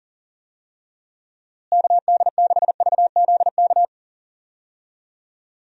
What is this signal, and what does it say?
Morse code, heard as a keyed tone, about 40 wpm, 700 Hz. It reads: KD6VZK
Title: USA Call Signs - 6 Characters - Repeated 3x - 40wpm